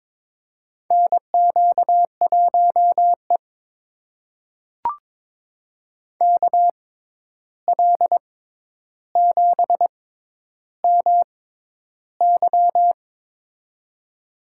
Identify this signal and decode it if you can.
Morse code, heard as a keyed tone, about 22 wpm, spaced as for 11 wpm, 700 Hz. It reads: NQ1E KL7MY
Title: USA Call Signs - 6x Character Spacing - 22wpm